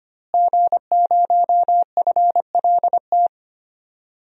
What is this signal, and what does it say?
Morse code, heard as a keyed tone, about 25 wpm, 700 Hz. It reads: G0FLT
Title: DXCC-100 Call Signs - 5 Characters - Repeated 2x - 25wpm